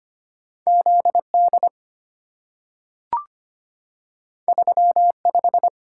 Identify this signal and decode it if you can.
Morse code, heard as a keyed tone, about 25 wpm, 700 Hz. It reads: Z D 3 5
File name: Two Letter-Number Combinations - 25wpm